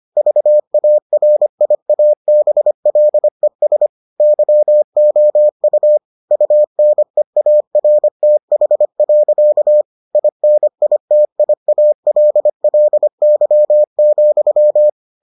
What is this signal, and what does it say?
Morse code, heard as a keyed tone, about 25 wpm, 600 Hz. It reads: VARIABLES YOU UNEARTH. INITIALLY,